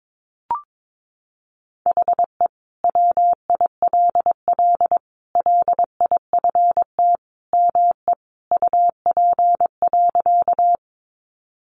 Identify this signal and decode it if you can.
Morse code, heard as a keyed tone, about 22 wpm, 700 Hz. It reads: HE WILL LIFT ME UP.
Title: Sentences from Top 1000 Words 22wpm